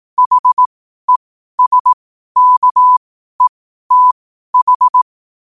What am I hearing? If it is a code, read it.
Morse code, heard as a keyed tone, about 18 wpm, spaced as for 12 wpm, 1000 Hz. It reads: HESKETH
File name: qotdfiles / 12wpm